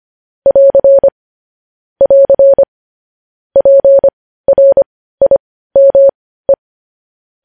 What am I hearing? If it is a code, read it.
Morse code, heard as a keyed tone, about 25 wpm, spaced as for 15 wpm, 550 Hz. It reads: <AR> <AR> PRIME
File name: podcast / News headlines-1x-25x15